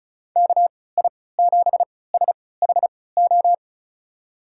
Morse code, heard as a keyed tone, about 36 wpm, spaced as for 20 wpm, 700 Hz. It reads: KI7SHO
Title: USA Call Signs - 6 Characters - 3x Character Spacing - 35wpm